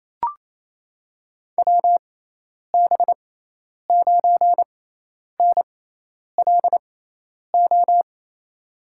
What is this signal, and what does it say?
Morse code, heard as a keyed tone, about 28 wpm, spaced as for 10 wpm, 700 Hz. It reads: WB9NLO
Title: USA Call Signs - 6x Character Spacing - 28wpm